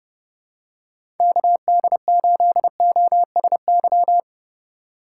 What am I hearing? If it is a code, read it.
Morse code, heard as a keyed tone, about 30 wpm, 700 Hz. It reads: KD8OSY
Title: USA Call Signs - 6 Characters - 2x Character Spacing - 30wpm